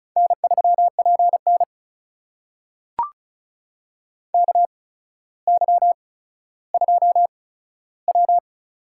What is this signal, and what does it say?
Morse code, heard as a keyed tone, about 36 wpm, spaced as for 15 wpm, 700 Hz. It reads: N3PN KY2W
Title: USA Call Signs - 4 Characters - 8x Character Spacing - 35wpm